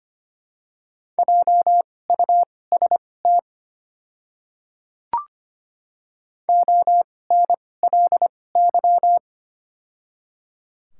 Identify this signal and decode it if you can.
Morse code, heard as a keyed tone, about 25 wpm, spaced as for 18 wpm, 700 Hz. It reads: JUST ONLY
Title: 2x Character Spacing - Top 300 Words - 25wpm